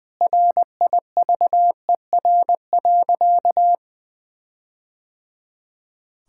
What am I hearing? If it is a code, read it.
Morse code, heard as a keyed tone, about 21 wpm, 700 Hz. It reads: RIVER.
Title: Sentences with 4-Word Phrases - 20wpm